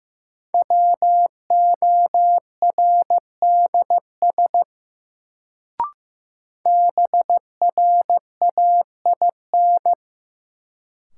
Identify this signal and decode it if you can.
Morse code, heard as a keyed tone, about 15 wpm, 700 Hz. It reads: WORDS BRAIN
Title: Sentences from Top 1000 Words - Review Part 6 - New Word Drill - 15wpm